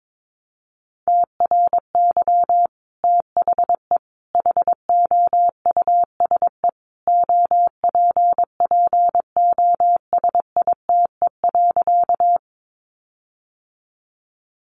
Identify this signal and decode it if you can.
Morse code, heard as a keyed tone, about 22 wpm, 700 Hz. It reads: TRY THE HOUSE OPPOSITE.
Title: Sentences from Top 1300 Words 22wpm